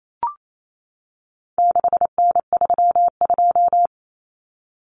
Morse code, heard as a keyed tone, about 28 wpm, 700 Hz. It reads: 6N32
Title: Common Cut Numbers - 4-Digit 28wpm